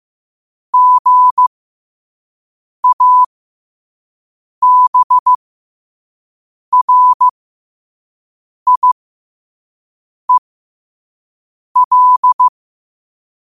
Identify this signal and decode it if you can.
Morse code, heard as a keyed tone, about 15 wpm, spaced as for 5 wpm, 1000 Hz. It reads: GABRIEL
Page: Quotes for Fri, 15 Aug 2025 in Morse Code at 5 words per minute.